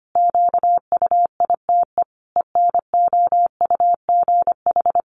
Morse code, heard as a keyed tone, about 25 wpm, 700 Hz. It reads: QUITE ENOUGH
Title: Sentences from Top 700 Words 25wpm